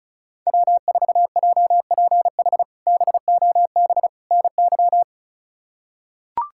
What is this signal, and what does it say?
Morse code, heard as a keyed tone, about 34 wpm, 700 Hz. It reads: W4JPH BOB NY
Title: K1USN SST QSO Practice - 35wpm